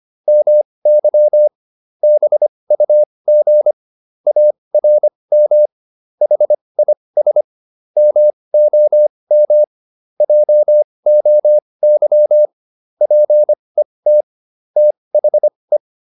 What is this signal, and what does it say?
Morse code, heard as a keyed tone, about 25 wpm, spaced as for 20 wpm, 600 Hz. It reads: MY BUG ARM HIS MOM JOY PET THE